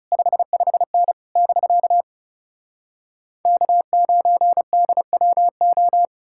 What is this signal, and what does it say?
Morse code, heard as a keyed tone, about 37 wpm, 700 Hz. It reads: 55N <BK> K9DWO
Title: Speed-Racing Common QSO Phrases Encore - 30wpm